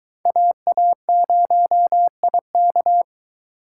Morse code, heard as a keyed tone, about 24 wpm, 700 Hz. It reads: AA0IK